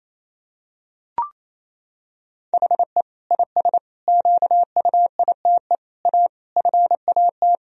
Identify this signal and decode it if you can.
Morse code, heard as a keyed tone, about 28 wpm, 700 Hz. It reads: HE IS QUITE A FAT
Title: Sentences from Top 1000 Words - Review Part 4 - 28wpm